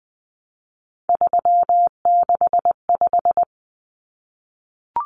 Morse code, heard as a keyed tone, about 20 wpm, 700 Hz. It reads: 365
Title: Common Cut Numbers - 3-Digit 20wpm